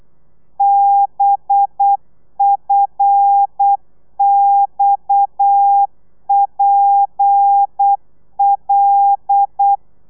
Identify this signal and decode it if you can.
Morse code, heard as a keyed tone, about 8 wpm, 800 Hz. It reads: BFXPL